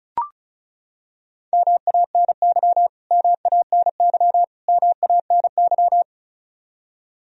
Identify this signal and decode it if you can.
Morse code, heard as a keyed tone, about 36 wpm, 700 Hz. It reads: MANY MANY MANY
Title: Top 100 Words Repeated 35wpm